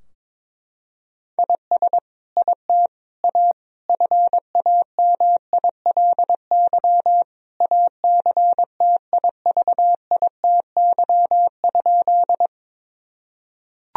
Morse code, heard as a keyed tone, about 22 wpm, 700 Hz. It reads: IS IT A FAMILY ACTIVITY?
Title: Sentences from Top 1000 Words - Review Part 4 - 22wpm